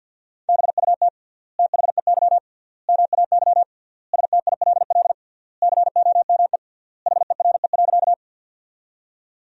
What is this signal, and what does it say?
Morse code, heard as a keyed tone, about 50 wpm, 700 Hz. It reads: BUT THEY MAY STILL COME HERE.
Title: 3x Word-Spacing - Sentences from Top 300 Words - 50wpm